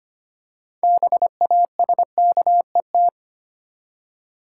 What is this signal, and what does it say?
Morse code, heard as a keyed tone, about 25 wpm, 700 Hz. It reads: BASKET
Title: Six Letter Words - 25wpm